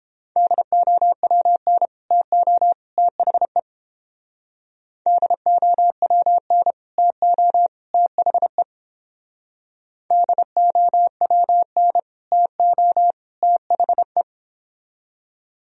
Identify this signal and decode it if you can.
Morse code, heard as a keyed tone, about 30 wpm, 700 Hz. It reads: DOWN TO THE DOWN TO THE DOWN TO THE